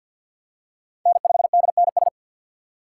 Morse code, heard as a keyed tone, about 53 wpm, 690 Hz. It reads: N5DNS
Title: USA Call Signs - 5 Characters - 3x Character Spacing - 50wpm